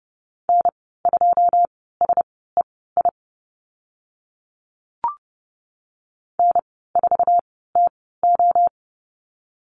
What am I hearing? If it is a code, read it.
Morse code, heard as a keyed tone, about 30 wpm, spaced as for 17 wpm, 700 Hz. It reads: N2SEI N4TO
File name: Lesson 14g - Number 2 - Call Signs 30wpm